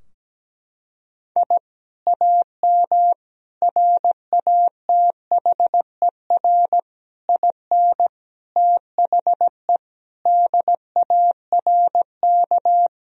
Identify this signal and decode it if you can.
Morse code, heard as a keyed tone, about 17 wpm, 700 Hz. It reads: I AM RATHER IN THE DARK
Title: Sentences from Top 1000 Words - Review Part 4 - 17wpm